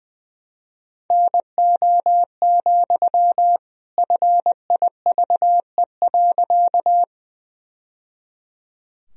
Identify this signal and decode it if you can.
Morse code, heard as a keyed tone, about 20 wpm, 700 Hz. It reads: NO, FIVE.
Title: Sentences from Top 1000 Words - Review Part 2 - 20wpm